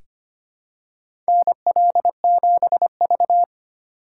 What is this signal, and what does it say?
Morse code, heard as a keyed tone, about 25 wpm, 700 Hz. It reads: NL7V
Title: K1USN SST QSO Practice - 25wpm